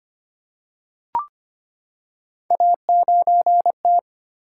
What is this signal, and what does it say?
Morse code, heard as a keyed tone, about 25 wpm, 700 Hz. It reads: A9T
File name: Three Letter-Number Combinations - 25wpm